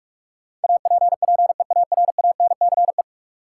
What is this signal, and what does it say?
Morse code, heard as a keyed tone, about 46 wpm, 700 Hz. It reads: APPEARANCE
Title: Sentences from Top 1100 Words - New Word Drill - 45wpm